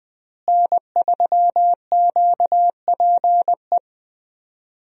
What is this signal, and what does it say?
Morse code, heard as a keyed tone, about 20 wpm, 700 Hz. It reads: N3QPE
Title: USA Call Signs - 5 Characters - Repeated 2x - 20wpm